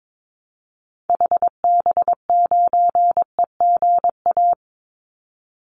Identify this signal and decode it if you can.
Morse code, heard as a keyed tone, about 22 wpm, 700 Hz. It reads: HB9EGA
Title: Call Signs - 22wpm